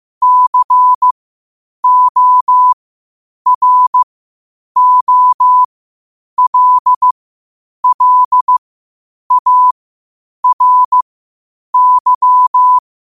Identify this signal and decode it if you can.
Morse code, heard as a keyed tone, about 15 wpm, spaced as for 8 wpm, 1000 Hz. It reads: COROLLARY
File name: qotdfiles / 8wpm